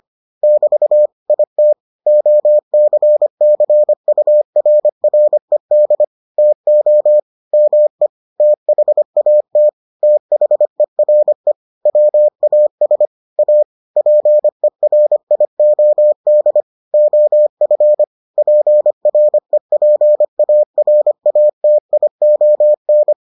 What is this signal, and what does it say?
Morse code, heard as a keyed tone, about 25 wpm, 600 Hz. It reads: <BT> IT OCCURRED TO ME THAT THERE WAS A PERIOD OF PREPARATION